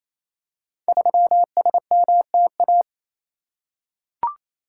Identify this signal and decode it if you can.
Morse code, heard as a keyed tone, about 28 wpm, 700 Hz. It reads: 3SMTA
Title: Code Groups - 5 Letter:Numbers - 28wpm